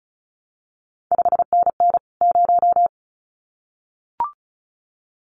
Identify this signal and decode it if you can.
Morse code, heard as a keyed tone, about 36 wpm, 700 Hz. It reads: 5NN 0
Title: CQ WW DX Contest Exchange Encore - 35wpm